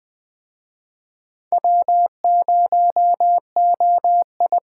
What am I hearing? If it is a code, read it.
Morse code, heard as a keyed tone, about 20 wpm, 700 Hz. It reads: W0OI
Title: Speed Racing - USA Call Signs 17wpm